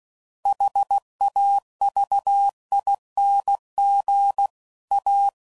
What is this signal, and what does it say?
Morse code, heard as a keyed tone, about 16 wpm, 800 Hz. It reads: HAVING A